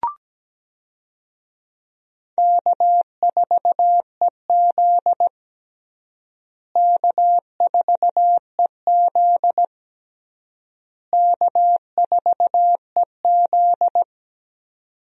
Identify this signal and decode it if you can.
Morse code, heard as a keyed tone, about 17 wpm, 700 Hz. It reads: K4EZ K4EZ K4EZ
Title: USA Call Signs - 4 Characters - Repeated 3x - 17wpm